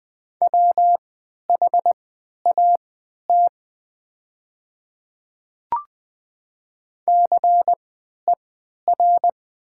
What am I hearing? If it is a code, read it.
Morse code, heard as a keyed tone, about 20 wpm, spaced as for 11 wpm, 700 Hz. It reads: WHAT CER
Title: 3x Character Spacing - Top 500 Words - 20wpm